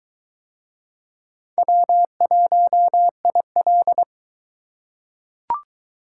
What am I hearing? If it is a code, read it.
Morse code, heard as a keyed tone, about 22 wpm, 700 Hz. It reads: W1IL